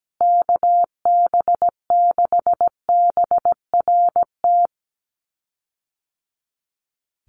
Morse code, heard as a keyed tone, about 17 wpm, 700 Hz. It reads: KB6BRT
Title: USA Call Signs Repeated 3x 17wpm